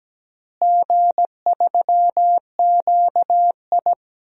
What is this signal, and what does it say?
Morse code, heard as a keyed tone, about 17 wpm, 700 Hz. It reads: G3QI
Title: DXCC-100 Call Signs - 4 Characters - Repeated 3x - 17wpm